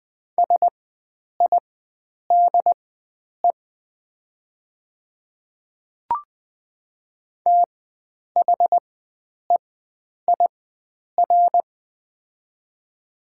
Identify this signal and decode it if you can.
Morse code, heard as a keyed tone, about 20 wpm, spaced as for 9 wpm, 700 Hz. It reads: SIDE THEIR
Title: 4x Character Spacing - Top 100 Words - 20wpm